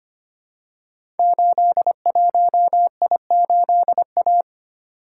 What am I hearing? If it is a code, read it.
Morse code, heard as a keyed tone, about 25 wpm, 700 Hz. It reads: 81I8A
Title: Code Groups - 5 Letter:Numbers - 25wpm